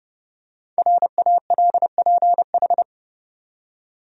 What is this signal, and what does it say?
Morse code, heard as a keyed tone, about 31 wpm, 700 Hz. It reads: RALPH
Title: USA Common Names - Max 5 Letters - 30wpm